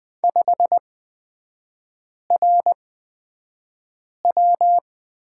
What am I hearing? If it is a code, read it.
Morse code, heard as a keyed tone, about 20 wpm, spaced as for 5 wpm, 690 Hz. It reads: 5RW
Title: 229 Lesson 37e - Letter x - Three Characters 20wpm